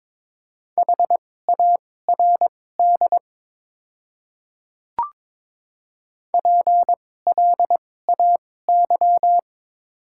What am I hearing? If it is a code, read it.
Morse code, heard as a keyed tone, about 22 wpm, spaced as for 16 wpm, 700 Hz. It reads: HARD PLAY
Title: 2x Character Spacing - Top 300 Words - 22wpm